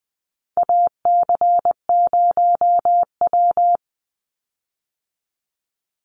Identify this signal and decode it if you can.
Morse code, heard as a keyed tone, about 20 wpm, 700 Hz. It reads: AC0W
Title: Call Signs Repeated 2x - 20wpm